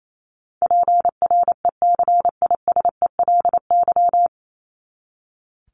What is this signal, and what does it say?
Morse code, heard as a keyed tone, about 28 wpm, 700 Hz. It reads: PRECISELY
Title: LY Suffix - Top 100 Words with Base Words 28wpm